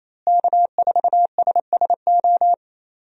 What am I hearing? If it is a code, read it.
Morse code, heard as a keyed tone, about 27 wpm, 700 Hz. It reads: K4SSO
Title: USA Call Signs - 5 Characters - Repeated 2x - 28wpm